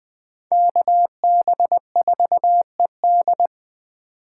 Morse code, heard as a keyed tone, about 20 wpm, 700 Hz. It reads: KB4ED
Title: Speed Racing - USA 2x2 Call Signs 17wpm